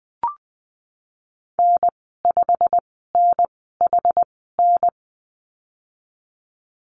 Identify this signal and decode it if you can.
Morse code, heard as a keyed tone, about 20 wpm, spaced as for 14 wpm, 700 Hz. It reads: N5NHN
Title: 219 Lesson 35g - Number 0 - Call Signs 20wpm